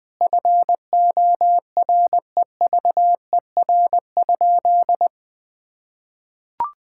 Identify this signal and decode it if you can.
Morse code, heard as a keyed tone, about 20 wpm, 700 Hz. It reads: FOREVER?
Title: Sentences from Top 1600 Words 20wpm